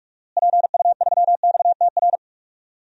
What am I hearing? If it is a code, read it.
Morse code, heard as a keyed tone, about 46 wpm, 700 Hz. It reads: PU3XTR
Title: DXCC-100 Call Signs - 6 Characters - Repeated 2x - 45wpm